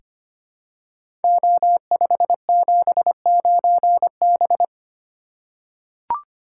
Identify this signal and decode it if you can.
Morse code, heard as a keyed tone, about 25 wpm, 700 Hz. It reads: O579B
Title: Code Groups - 5 Letter:Numbers - 25wpm